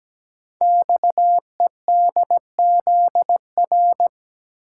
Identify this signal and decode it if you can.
Morse code, heard as a keyed tone, about 17 wpm, 700 Hz. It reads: XEDZR
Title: Code Groups - 5 Letters - 17wpm